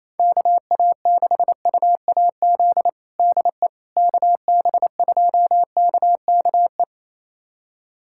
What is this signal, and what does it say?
Morse code, heard as a keyed tone, about 28 wpm, 700 Hz. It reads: KA6UAZ DE KB2KKE
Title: USA Callsign DE Callsign - 6 Characters - 28wpm